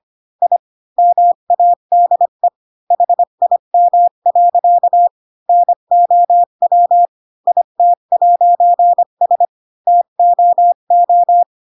Morse code, heard as a keyed tone, about 25 wpm, 700 Hz. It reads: I MADE HIM. NOW IT'S TOO
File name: Pinocchio / Pinocchio ch3 20 25 ik1hge